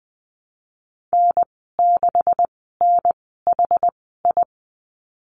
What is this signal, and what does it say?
Morse code, heard as a keyed tone, about 20 wpm, spaced as for 15 wpm, 700 Hz. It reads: N6NHI
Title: 147 Lesson 24g - Letter y - Call Signs 20wpm